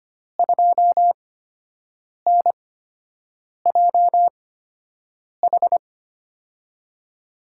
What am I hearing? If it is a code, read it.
Morse code, heard as a keyed tone, about 25 wpm, spaced as for 7 wpm, 700 Hz. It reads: 2NJH
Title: USA Call Signs - 5 Characters - 8x Character Spacing - 25wpm